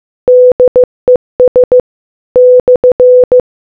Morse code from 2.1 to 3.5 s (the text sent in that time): /